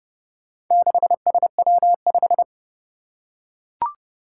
Morse code, heard as a keyed tone, about 30 wpm, 700 Hz. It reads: 6SW5
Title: Four Challenging Letter-Number Combinations 30wpm